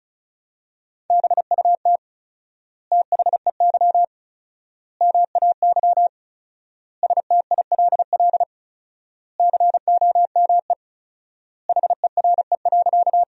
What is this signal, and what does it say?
Morse code, heard as a keyed tone, about 34 wpm, 700 Hz. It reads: BUT THEY MAY STILL COME HERE.
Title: 4x Word-Spacing - Sentences from Top 300 Words - 35wpm